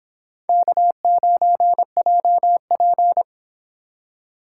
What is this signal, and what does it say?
Morse code, heard as a keyed tone, about 30 wpm, 700 Hz. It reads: K9JP